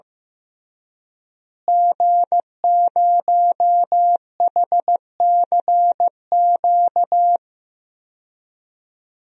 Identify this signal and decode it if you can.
Morse code, heard as a keyed tone, about 16 wpm, 700 Hz. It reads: G0HCQ
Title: DXCC-100 Call Signs - 5 Characters - Repeated 3x - 15wpm